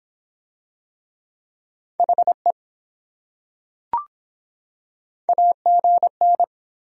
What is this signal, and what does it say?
Morse code, heard as a keyed tone, about 23 wpm, 700 Hz. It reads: HE AGN